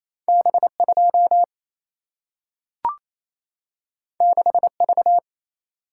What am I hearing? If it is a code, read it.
Morse code, heard as a keyed tone, about 28 wpm, 700 Hz. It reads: B 2 6 V
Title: Single-Letter Single-Number 28wpm